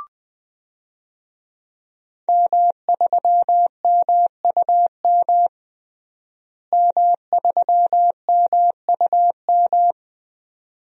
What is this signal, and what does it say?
Morse code, heard as a keyed tone, about 20 wpm, 700 Hz. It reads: M3MUM M3MUM
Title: DXCC-100 Call Signs - 5 Characters - Repeated 2x - 20wpm